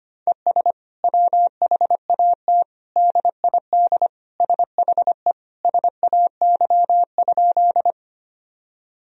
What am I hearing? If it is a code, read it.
Morse code, heard as a keyed tone, about 25 wpm, 700 Hz. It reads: ES WHAT DID SHE SAY?
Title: Sentences from Top 200 Words and Abbreviations 25wpm